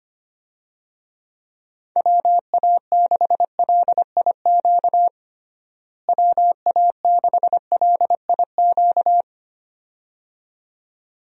Morse code, heard as a keyed tone, about 25 wpm, 700 Hz. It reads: WA6LIQ WA6LIQ
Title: USA Call Signs - 6 Characters - Repeated 2x - 25wpm